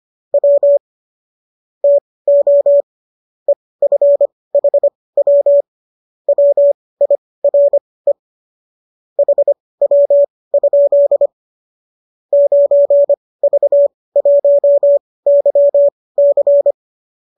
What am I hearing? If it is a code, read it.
Morse code, heard as a keyed tone, about 25 wpm, spaced as for 18 wpm, 570 Hz. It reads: W TO EFHW WIRE HW? 9V1YC